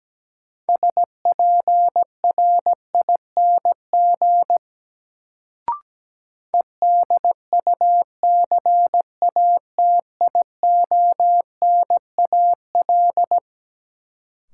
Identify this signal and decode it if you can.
Morse code, heard as a keyed tone, about 17 wpm, 700 Hz. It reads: SPRING EDUCATIONAL
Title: Sentences from Top 1200 Words - New Word Drill Alt - 17wpm